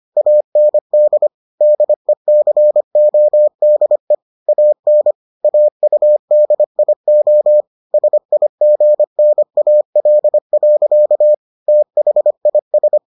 Morse code, heard as a keyed tone, about 25 wpm, 600 Hz. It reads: AND DECODE AN AUDIO SIGNAL. THIS